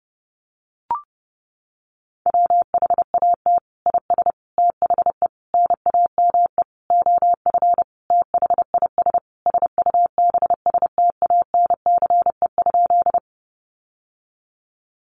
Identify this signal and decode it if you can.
Morse code, heard as a keyed tone, about 30 wpm, 700 Hz. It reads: WHAT IS THE NAME OF THIS SUBSTANCE?
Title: Sentences from Top 1500 Words 30wpm